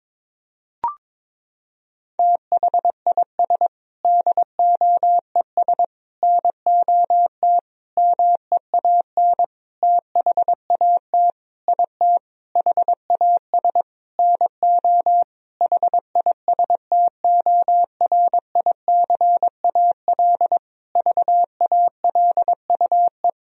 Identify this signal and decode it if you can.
Morse code, heard as a keyed tone, about 22 wpm, 700 Hz. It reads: THIS DOES NOT MEAN THAT IT HAS NO HISTORICAL VALUE
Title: Sentences from Top 1000 Words - Review Part 4 - 22wpm